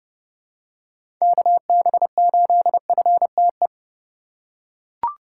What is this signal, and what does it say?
Morse code, heard as a keyed tone, about 30 wpm, 700 Hz. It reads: KB8FTE
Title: USA Call Signs - 3x Character Spacing - 30wpm